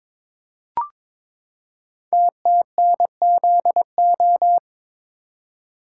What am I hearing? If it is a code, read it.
Morse code, heard as a keyed tone, about 22 wpm, 700 Hz. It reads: TTNZO
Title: Code Groups - 5 Letters - 22wpm